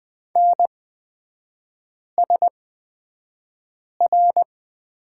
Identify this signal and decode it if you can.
Morse code, heard as a keyed tone, about 20 wpm, spaced as for 5 wpm, 690 Hz. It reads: NSR
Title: 229 Lesson 37e - Letter x - Three Characters 20wpm